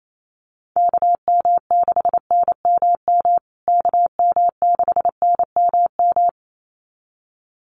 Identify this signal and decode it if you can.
Morse code, heard as a keyed tone, about 28 wpm, 700 Hz. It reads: KM6NMM KM6NMM
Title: USA Call Signs Repeated 2x 28wpm